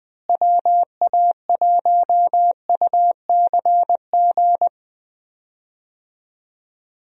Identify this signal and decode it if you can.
Morse code, heard as a keyed tone, about 20 wpm, 700 Hz. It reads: WA1UCG
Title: USA Call Signs - 6 Characters - Repeated 2x - 20wpm